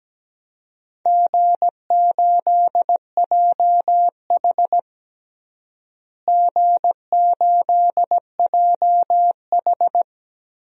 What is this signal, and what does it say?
Morse code, heard as a keyed tone, about 17 wpm, 700 Hz. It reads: G8JH G8JH
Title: DXCC-100 Call Signs - 4 Characters - Repeated 3x - 17wpm